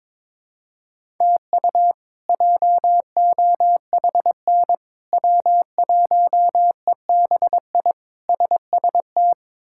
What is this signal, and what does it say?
Morse code, heard as a keyed tone, about 21 wpm, 700 Hz. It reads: TU JOHN W1EBI SST
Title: K1USN SST QSO Practice - 22wpm